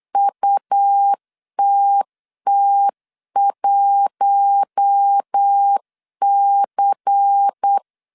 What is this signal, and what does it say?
Morse code, heard as a keyed tone, about 8 wpm, 800 Hz. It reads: UTT1C
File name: CW / DET / 1 9 1500C S A 8wpm 5637